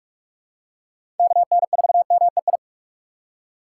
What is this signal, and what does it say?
Morse code, heard as a keyed tone, about 44 wpm, 700 Hz. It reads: KN4MEI
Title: USA Call Signs - 6 Characters - 6x Character Spacing - 45wpm